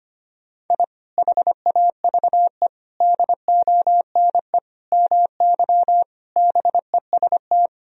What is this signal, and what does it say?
Morse code, heard as a keyed tone, about 25 wpm, 700 Hz. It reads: I HAVE DONE MY BEST
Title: Sentences from Top 1000 Words - Review Part 4 - 25wpm